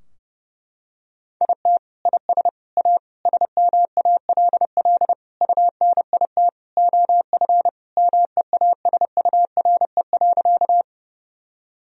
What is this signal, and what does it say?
Morse code, heard as a keyed tone, about 30 wpm, 700 Hz. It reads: IT IS A SMALL UNIT OF MEASURE.
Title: Sentences from Top 1000 Words - Review Part 2 - 30wpm